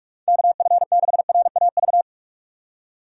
Morse code, heard as a keyed tone, about 46 wpm, 700 Hz. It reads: KF6RAV
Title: USA Call Signs - 6 Characters - 4x Character Spacing - 45wpm